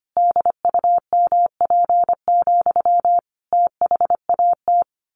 Morse code, heard as a keyed tone, about 25 wpm, 700 Hz. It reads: DUMP, THAT
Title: A Sound Of Thunder 25wpm